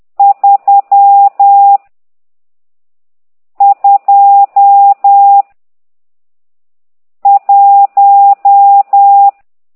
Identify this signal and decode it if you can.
Morse code, heard as a keyed tone, about 10 wpm, 800 Hz. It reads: 3 2 1
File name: ct2idw / exerc12